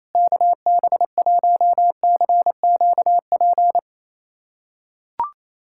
Morse code, heard as a keyed tone, about 28 wpm, 700 Hz. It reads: KB1CQP
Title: USA Callsign DE Callsign 28wpm